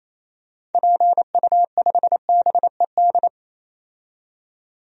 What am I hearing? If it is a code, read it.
Morse code, heard as a keyed tone, about 28 wpm, 700 Hz. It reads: PU5BED
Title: DXCC-100 Call Signs - 6 Characters - 28wpm